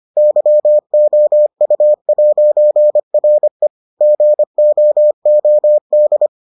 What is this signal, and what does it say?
Morse code, heard as a keyed tone, about 25 wpm, 600 Hz. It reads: YOU'RE GOOD